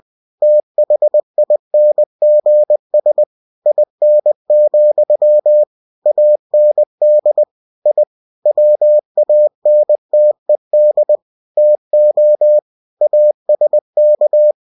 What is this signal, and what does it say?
Morse code, heard as a keyed tone, about 20 wpm, 600 Hz. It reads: THINGS IN, AND I WANTED TO ASK
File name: cw-books / winnie-the-pooh / winnie-20wpm-page-0051